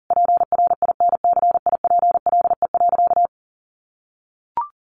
Morse code, heard as a keyed tone, about 40 wpm, 700 Hz. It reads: PRINCIPLE.
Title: Sentences from Top 1400 Words 40wpm